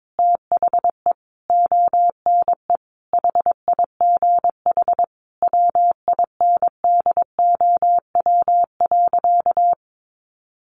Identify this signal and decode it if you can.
Morse code, heard as a keyed tone, about 22 wpm, 700 Hz. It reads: THE ONE HIGH WINDOW.
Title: A Sound Of Thunder 22wpm